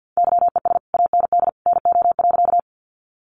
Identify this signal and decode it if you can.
Morse code, heard as a keyed tone, about 50 wpm, 700 Hz. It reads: YES AND NO.
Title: Sentences from Top 500 Words 50wpm